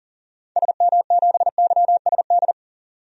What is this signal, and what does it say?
Morse code, heard as a keyed tone, about 40 wpm, 700 Hz. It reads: SM7YSD
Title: DXCC-100 Call Signs - 6 Characters - 40wpm